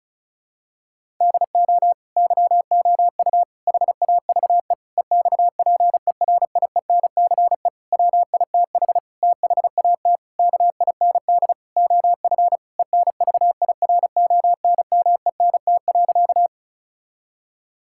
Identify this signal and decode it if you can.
Morse code, heard as a keyed tone, about 36 wpm, 700 Hz. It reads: DO YOU HAVE EXPERIENCE WITH THAT KIND OF ENVIRONMENT.
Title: Sentences from Top 1000 Words - Review Part 5 - 35wpm